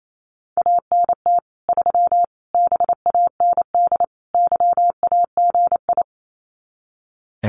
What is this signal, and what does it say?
Morse code, heard as a keyed tone, about 28 wpm, 700 Hz. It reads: ANT 3 BAND YAGI
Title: Common QSO Phrases 28wpm